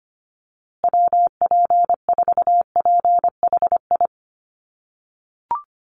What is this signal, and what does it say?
Morse code, heard as a keyed tone, about 25 wpm, 700 Hz. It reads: WP4PHI
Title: USA Call Signs Repeated 3x 25wpm